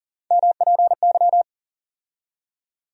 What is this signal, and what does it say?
Morse code, heard as a keyed tone, about 40 wpm, 700 Hz. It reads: MPY
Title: USA Call Sign Postfixes - 3 Letters 40wpm